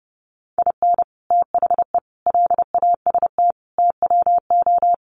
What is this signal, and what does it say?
Morse code, heard as a keyed tone, about 28 wpm, 700 Hz. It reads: IN THE LAST TWO
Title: Sets of 5 Words 30wpm